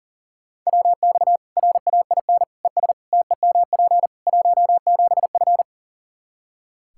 Morse code, heard as a keyed tone, about 37 wpm, 700 Hz. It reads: WX RAIN ES TEMP 17F
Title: Practice QSO - Exchange 2 - Info - 40wpm